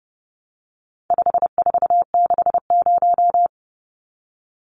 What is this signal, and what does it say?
Morse code, heard as a keyed tone, about 30 wpm, 700 Hz. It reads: 5460
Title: Numbers - 4-Digit 30wpm